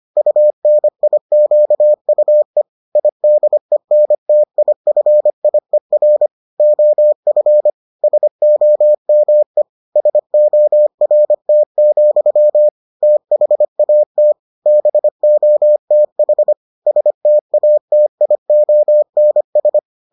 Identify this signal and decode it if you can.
Morse code, heard as a keyed tone, about 25 wpm, 600 Hz. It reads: UNIQUE IDENTIFIER OF SOME SORT, THAT BOTH STATIONS